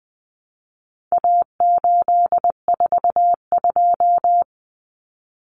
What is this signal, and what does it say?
Morse code, heard as a keyed tone, about 20 wpm, 700 Hz. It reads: A842
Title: Common Cut Numbers - 4-Digit 20wpm